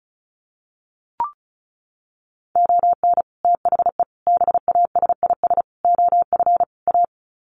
Morse code, heard as a keyed tone, about 34 wpm, 700 Hz. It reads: ON THE BASIS OF A
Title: Sets of 5 Words Encore 35wpm